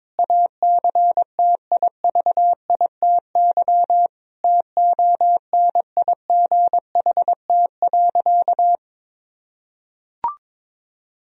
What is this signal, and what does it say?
Morse code, heard as a keyed tone, about 22 wpm, 700 Hz. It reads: ACTIVITY TONIGHT.
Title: Sentences from Top 1000 Words - Review Part 4 - 22wpm